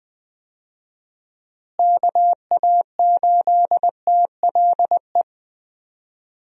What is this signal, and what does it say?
Morse code, heard as a keyed tone, about 20 wpm, 700 Hz. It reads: KA8TLE
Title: USA Call Signs - 6 Characters - Repeated 2x - 20wpm